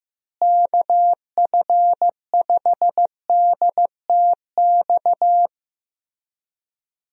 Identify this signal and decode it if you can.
Morse code, heard as a keyed tone, about 15 wpm, 700 Hz. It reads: KF5DTX
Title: USA Call Signs - 6 Characters - Repeated 3x - 15wpm